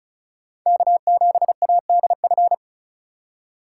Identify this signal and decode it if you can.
Morse code, heard as a keyed tone, about 34 wpm, 700 Hz. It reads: K7ADF
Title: USA Call Signs - 5 Characters - 4x Character Spacing - 35wpm